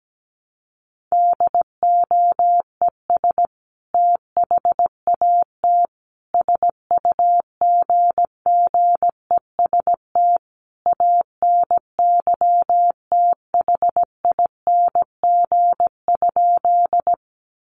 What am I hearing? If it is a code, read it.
Morse code, heard as a keyed tone, about 17 wpm, 700 Hz. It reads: DOES THAT SUGGEST ANYTHING?
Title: Sentences from Top 1300 Words 17wpm